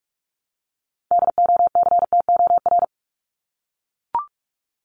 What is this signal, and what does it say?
Morse code, heard as a keyed tone, about 44 wpm, 700 Hz. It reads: DOCTOR
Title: Top 1251-1500 Words 45wpm